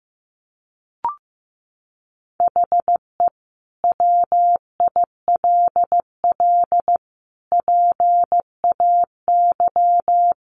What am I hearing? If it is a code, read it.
Morse code, heard as a keyed tone, about 15 wpm, 700 Hz. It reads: HE WILL PAY
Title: Sentences from Top 1400 Words 15wpm